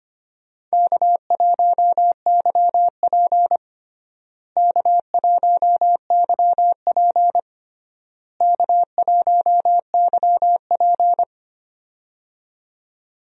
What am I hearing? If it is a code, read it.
Morse code, heard as a keyed tone, about 25 wpm, 700 Hz. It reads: K1YP K1YP K1YP
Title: USA Call Signs - 4 Characters - Repeated 3x - 25wpm